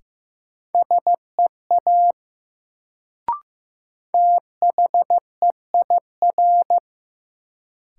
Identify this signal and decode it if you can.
Morse code, heard as a keyed tone, about 16 wpm, 700 Hz. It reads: SEA THEIR
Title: Sentences from Top 300 Words - New Word Drill - 15wpm